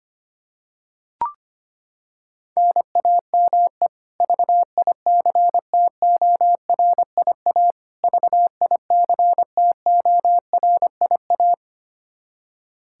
Morse code, heard as a keyed tone, about 25 wpm, 700 Hz. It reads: NAME VICTORIA VICTORIA
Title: K1USN SST QSO Practice 25wpm